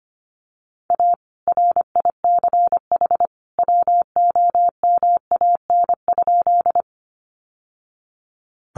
Morse code, heard as a keyed tone, about 25 wpm, 700 Hz. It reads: A RICH WOMAN?
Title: Sentences from Top 1100 Words 25wpm